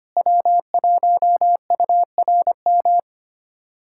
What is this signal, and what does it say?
Morse code, heard as a keyed tone, about 25 wpm, 700 Hz. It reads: W1URM
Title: USA Callsign DE Callsign - 5 Characters - 25wpm